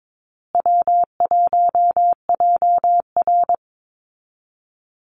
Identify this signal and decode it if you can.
Morse code, heard as a keyed tone, about 22 wpm, 700 Hz. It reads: W1JR
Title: Call Signs Repeated 2x - 22wpm